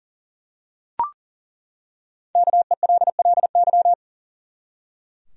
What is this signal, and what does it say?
Morse code, heard as a keyed tone, about 40 wpm, 700 Hz. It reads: KELLY
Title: USA Common Names - Max 5 Letters - 40wpm